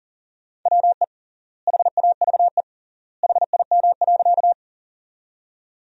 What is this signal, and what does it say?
Morse code, heard as a keyed tone, about 40 wpm, 700 Hz. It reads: WE HAVE HIM.
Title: 3x Word-Spacing - Sentences from Top 100 Words - 40wpm